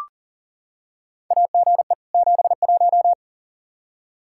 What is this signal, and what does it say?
Morse code, heard as a keyed tone, about 40 wpm, 690 Hz. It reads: AGE 71
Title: Practice QSO - Exchange 3 - More Info - 40wpm